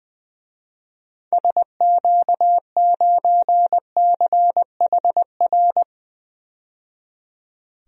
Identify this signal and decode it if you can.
Morse code, heard as a keyed tone, about 20 wpm, 700 Hz. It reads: SQ9CHR
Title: DXCC-100 Call Signs - 6 Characters - 20wpm